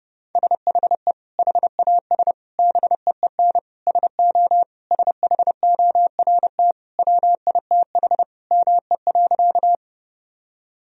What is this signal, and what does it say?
Morse code, heard as a keyed tone, about 30 wpm, 700 Hz. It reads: SHE HAS BEEN SO SHORT WITH ME.
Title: Sentences from Top 1000 Words - Review Part 2 - 30wpm